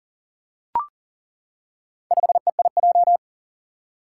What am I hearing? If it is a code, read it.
Morse code, heard as a keyed tone, about 40 wpm, 700 Hz. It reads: 5EIJ
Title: Four Challenging Letter-Number Combinations 40wpm